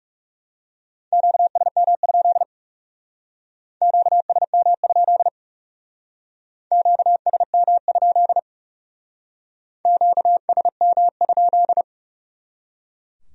Morse code, heard as a keyed tone, about 37 wpm, 700 Hz. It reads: QSM? QSM? QSM? QSM?
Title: Speed-Racing Q-Codes - 30wpm